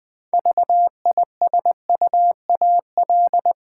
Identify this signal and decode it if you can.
Morse code, heard as a keyed tone, about 21 wpm, 700 Hz. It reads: VISUAL
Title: Six Letter Words - 20wpm